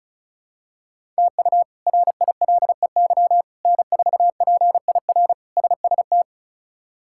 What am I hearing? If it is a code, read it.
Morse code, heard as a keyed tone, about 36 wpm, 700 Hz. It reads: TU RILEY N4PIR SST
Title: K1USN SST QSO Practice - 35wpm